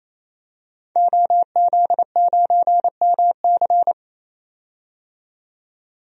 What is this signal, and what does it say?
Morse code, heard as a keyed tone, about 28 wpm, 700 Hz. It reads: OZ9MC
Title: DXCC-100 Call Signs - 5 Characters - 28wpm